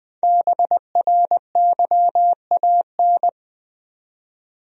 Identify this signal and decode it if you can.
Morse code, heard as a keyed tone, about 20 wpm, 700 Hz. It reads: BRYAN
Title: USA Common Names - Max 5 Letters - 20wpm